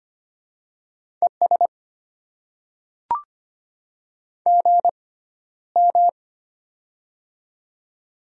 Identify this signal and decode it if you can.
Morse code, heard as a keyed tone, about 25 wpm, spaced as for 13 wpm, 700 Hz. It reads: ES GM
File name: Top 20 QSO Elements - 6x Character Spacing - 25wpm